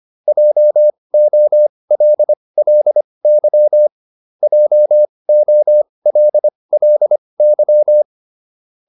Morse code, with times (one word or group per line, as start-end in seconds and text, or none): 0.3-3.9 JOLLY
4.4-8.0 JOLLY